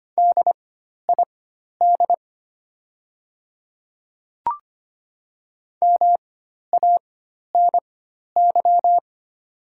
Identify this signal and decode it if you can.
Morse code, heard as a keyed tone, about 25 wpm, spaced as for 12 wpm, 700 Hz. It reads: DID MANY
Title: 4x Character Spacing - Top 200 Words - 25wpm